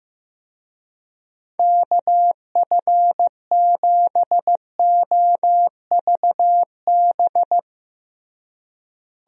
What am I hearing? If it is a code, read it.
Morse code, heard as a keyed tone, about 16 wpm, 700 Hz. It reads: KF7OVB
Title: USA Call Signs - 6 Characters - Repeated 2x - 15wpm